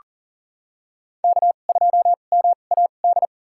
Morse code, heard as a keyed tone, about 40 wpm, spaced as for 29 wpm, 700 Hz. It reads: K2MAD
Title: USA Call Signs - 5 Characters - 2x Character Spacing - 40wpm